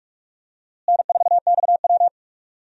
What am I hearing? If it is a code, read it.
Morse code, heard as a keyed tone, about 46 wpm, 700 Hz. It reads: N4XW
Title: USA Call Signs - 4 Characters - Repeated 2x - 45wpm